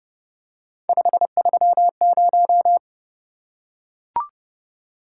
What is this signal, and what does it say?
Morse code, heard as a keyed tone, about 29 wpm, 700 Hz. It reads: 530